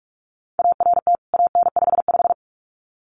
Morse code, heard as a keyed tone, about 44 wpm, 700 Hz. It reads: ART AN55
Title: CWT Exchange 45wpm